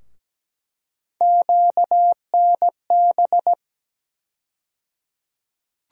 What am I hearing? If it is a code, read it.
Morse code, heard as a keyed tone, about 17 wpm, 700 Hz. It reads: QNB
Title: USA Call Sign Postfixes - 3 Letters 17wpm